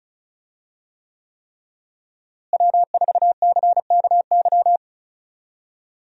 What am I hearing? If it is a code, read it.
Morse code, heard as a keyed tone, about 36 wpm, 700 Hz. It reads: W4CKY
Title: USA Call Signs - 5 Characters - Repeated 3x - 35wpm